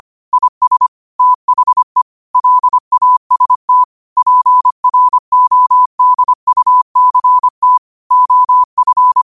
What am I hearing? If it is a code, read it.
Morse code, heard as a keyed tone, about 25 wpm, 1000 Hz. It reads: IS THE LAST PRODUCT OF
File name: qotdfiles / 25wpm